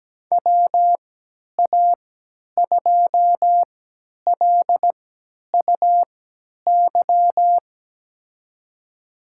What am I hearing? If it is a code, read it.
Morse code, heard as a keyed tone, about 17 wpm, spaced as for 10 wpm, 700 Hz. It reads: WA2LUY
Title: USA Call Signs - 6 Characters - 3x Character Spacing - 17wpm